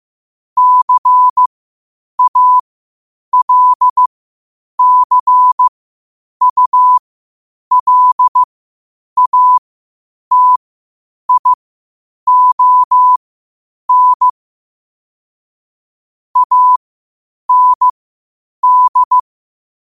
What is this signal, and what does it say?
Morse code, heard as a keyed tone, about 15 wpm, spaced as for 8 wpm, 1000 Hz. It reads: CALCULATION AND
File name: qotdfiles / 8wpm